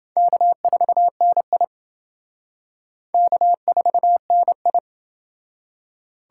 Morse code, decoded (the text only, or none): K4NI K4NI